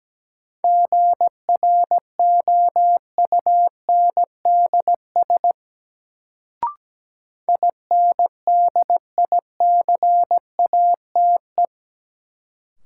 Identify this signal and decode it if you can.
Morse code, heard as a keyed tone, about 17 wpm, 700 Hz. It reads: GROUNDS INDICATE
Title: Sentences from Top 1300 Words - New Word Drill Alt - 17wpm